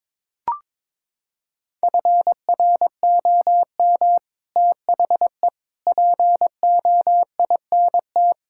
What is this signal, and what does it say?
Morse code, heard as a keyed tone, about 21 wpm, 700 Hz. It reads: FROM THE POINT
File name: practice / 22wpm / Sets of 6 Words - 22wpm